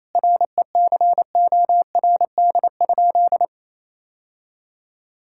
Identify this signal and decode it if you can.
Morse code, heard as a keyed tone, about 28 wpm, 700 Hz. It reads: RECORD?
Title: Sentences from Top 1000 Words - Review Part 6 - 28wpm